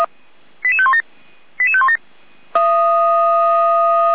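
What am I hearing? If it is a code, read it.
Five-tone selective call (ZVEI), heard as quick runs of five tones, tones of about 70 ms. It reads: 88517 88517
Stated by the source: ZVEI five-tone selective call received on an SDR, fire brigade.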